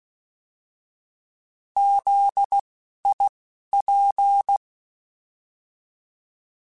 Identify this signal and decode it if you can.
Morse code, heard as a keyed tone, about 16 wpm, spaced as for 11 wpm, 800 Hz. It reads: ZIP
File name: Alphabetical / Alphabetical mp3s / Zs @10